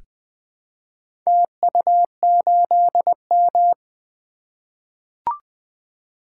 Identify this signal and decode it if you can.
Morse code, heard as a keyed tone, about 20 wpm, 700 Hz. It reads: TU8M
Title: Four Letter-Number Combinations - 20wpm